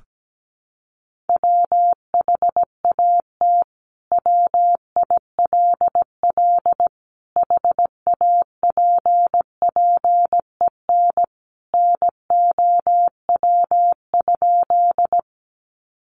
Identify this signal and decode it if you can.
Morse code, heard as a keyed tone, about 17 wpm, 700 Hz. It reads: WHAT WILL HAPPEN NOW?
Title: Sentences from Top 400 Words 17wpm